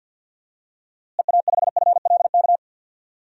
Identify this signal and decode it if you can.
Morse code, heard as a keyed tone, about 46 wpm, 700 Hz. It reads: EA5FLK
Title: DXCC-100 Call Signs - 6 Characters - Repeated 3x - 50wpm